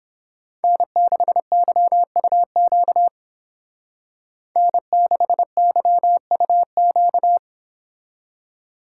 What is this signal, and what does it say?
Morse code, heard as a keyed tone, about 28 wpm, 700 Hz. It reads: N6YUQ N6YUQ